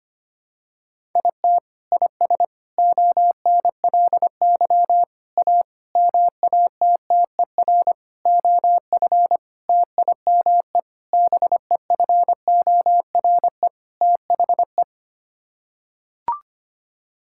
Morse code, heard as a keyed tone, about 25 wpm, 700 Hz. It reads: IT IS ONLY A MATTER OF TIME BEFORE THE
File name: Sets of 9 Words - 25wpm